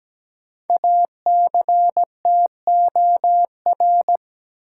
Morse code, heard as a keyed tone, about 17 wpm, 700 Hz. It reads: ACTOR
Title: Sentences from Top 1500 Words - New Word Drill - 17wpm